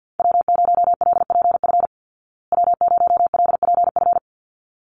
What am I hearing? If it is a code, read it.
Morse code, heard as a keyed tone, about 50 wpm, 700 Hz. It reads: W0LPF W0LPF
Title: Call Signs Repeated 3x - 50wpm